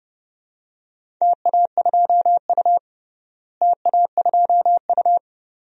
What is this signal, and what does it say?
Morse code, heard as a keyed tone, about 30 wpm, 700 Hz. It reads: TA2U TA2U
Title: DXCC-100 Call Signs - 4 Characters - Repeated 2x - 30wpm